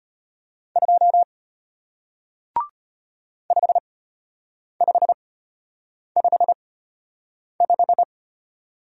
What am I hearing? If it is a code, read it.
Morse code, heard as a keyed tone, about 34 wpm, spaced as for 8 wpm, 700 Hz. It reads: 2 5555